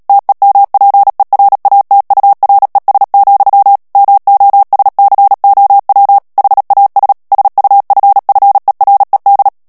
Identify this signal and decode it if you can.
Morse code, heard as a keyed tone, about 37 wpm, 800 Hz. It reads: TEMPERATURES, MOSCOW HAS SUFFERED
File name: morse mp3 / ANGLAIS texte clair 2 vitesse 35 mots minute